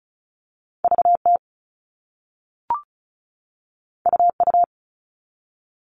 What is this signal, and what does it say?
Morse code, heard as a keyed tone, about 36 wpm, 700 Hz. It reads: VT UU
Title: All Cut Numbers - 2-Digit 35wpm